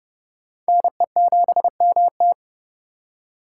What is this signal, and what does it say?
Morse code, heard as a keyed tone, about 38 wpm, 700 Hz. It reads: NE7MT